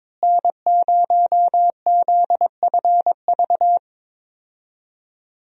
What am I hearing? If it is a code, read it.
Morse code, heard as a keyed tone, about 21 wpm, 700 Hz. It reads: N0ZFV
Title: USA Call Signs - 5 Characters - Repeated 2x - 22wpm